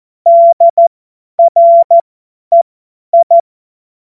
Morse code, heard as a keyed tone, about 14 wpm, spaced as for 10 wpm, 680 Hz. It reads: DREI
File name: Morsekurs / Witze / Witz 31. Januar 2023 50 F70